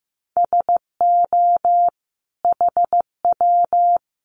Morse code, heard as a keyed tone, about 15 wpm, 700 Hz. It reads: SO HW
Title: Common QSO Phrases 15wpm